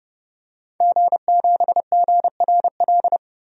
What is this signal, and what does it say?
Morse code, heard as a keyed tone, about 30 wpm, 700 Hz. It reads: G7GRL
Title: DXCC-100 Call Signs - 5 Characters - Repeated 2x - 30wpm